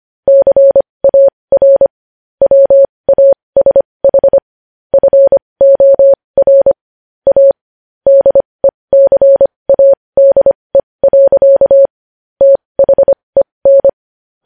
Morse code, heard as a keyed tone, about 25 wpm, spaced as for 20 wpm, 550 Hz. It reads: CAR WASH FOR A DECADE. THEN
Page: news-headlines-in-morse-code-at-20-wpm